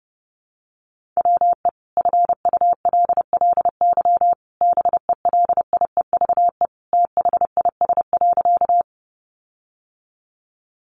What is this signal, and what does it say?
Morse code, heard as a keyed tone, about 30 wpm, 700 Hz. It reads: WE FULLY BELIEVE THIS.
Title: Sentences from Top 1300 Words 30wpm